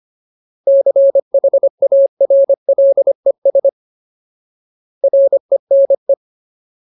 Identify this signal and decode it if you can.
Morse code, heard as a keyed tone, about 25 wpm, 550 Hz. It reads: CHARLES RENE
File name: Ultimate Word List / 25WPM Ultimate